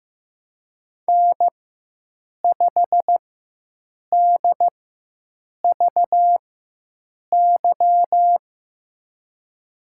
Morse code, heard as a keyed tone, about 15 wpm, spaced as for 7 wpm, 700 Hz. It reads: N5DVY
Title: USA Call Signs - 5 Characters - 4x Character Spacing - 15wpm